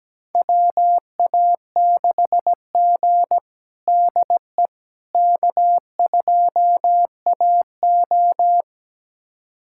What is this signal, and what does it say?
Morse code, heard as a keyed tone, about 17 wpm, 700 Hz. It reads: WA6G DE K2AO
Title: USA Callsign DE Callsign - 4 Characters - 17wpm